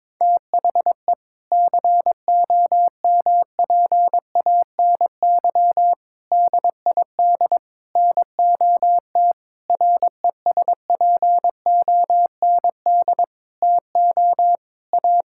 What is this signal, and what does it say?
Morse code, heard as a keyed tone, about 22 wpm, 700 Hz. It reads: THE COMPANY DID NOT RESPOND TO A
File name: practice / 22wpm / Sentences with 8-Word Phrases - 22wpm